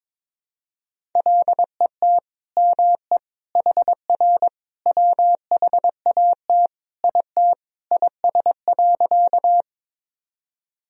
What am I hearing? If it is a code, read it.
Morse code, heard as a keyed tone, about 22 wpm, 700 Hz. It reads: LET ME HR WHAT IT IS.
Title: Sentences from Top 300 Words and Abbreviations 22wpm